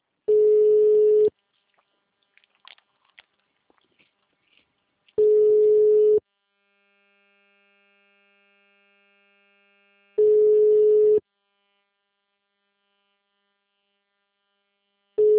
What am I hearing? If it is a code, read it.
Ringback tone (Europe, 425 Hz).